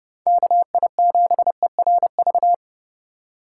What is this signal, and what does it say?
Morse code, heard as a keyed tone, about 31 wpm, 700 Hz. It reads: KI7ERV